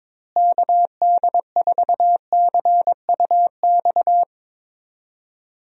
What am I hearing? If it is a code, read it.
Morse code, heard as a keyed tone, about 21 wpm, 700 Hz. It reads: KD4CUX
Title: USA Call Signs - 6 Characters - Repeated 2x - 22wpm